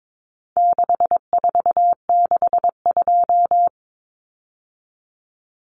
Morse code, heard as a keyed tone, about 22 wpm, 700 Hz. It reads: 6462
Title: Numbers - 4-Digit 22wpm